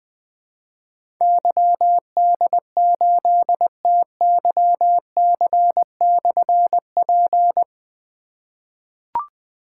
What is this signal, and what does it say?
Morse code, heard as a keyed tone, about 20 wpm, 700 Hz. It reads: YD8TYC/P
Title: Difficult Call Signs 20wpm